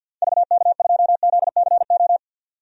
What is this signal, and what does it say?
Morse code, heard as a keyed tone, about 49 wpm, 700 Hz. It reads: VK2ZCO
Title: DXCC-100 Call Signs - 6 Characters - 50wpm